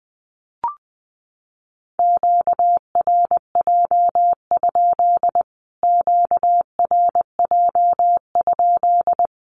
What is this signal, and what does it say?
Morse code, heard as a keyed tone, about 20 wpm, 700 Hz. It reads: QRJ? QRJ?
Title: Q-Codes Repeated 3x 20wpm